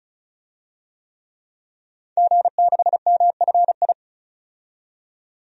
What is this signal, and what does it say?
Morse code, heard as a keyed tone, about 36 wpm, 700 Hz. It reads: G6MFI
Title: DXCC-100 Call Signs - 5 Characters - 35wpm